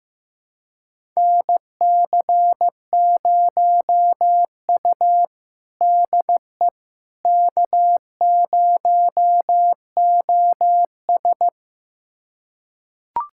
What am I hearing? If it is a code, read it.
Morse code, heard as a keyed tone, about 15 wpm, 700 Hz. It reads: NC0U DE K0OS
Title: USA Callsign DE Callsign - 4 Characters - 15wpm